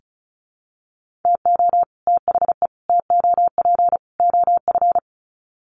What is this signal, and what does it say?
Morse code, heard as a keyed tone, about 34 wpm, 700 Hz. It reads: TO THE TOP OF
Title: Sets of 4 Words 35wpm